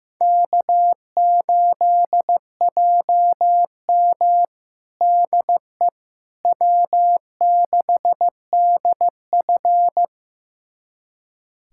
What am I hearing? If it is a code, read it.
Morse code, heard as a keyed tone, about 15 wpm, 700 Hz. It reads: K8JM DE W6DF
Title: USA Callsign DE Callsign - 4 Characters - 15wpm